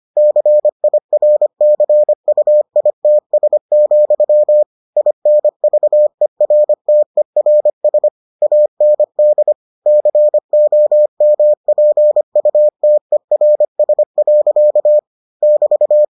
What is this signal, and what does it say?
Morse code, heard as a keyed tone, about 25 wpm, 600 Hz. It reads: CIRCUITS, INVERTERS AND COMPUTERS. <BT>